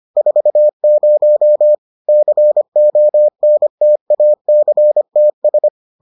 Morse code, heard as a keyed tone, about 25 wpm, 600 Hz. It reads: 40 CONTACTS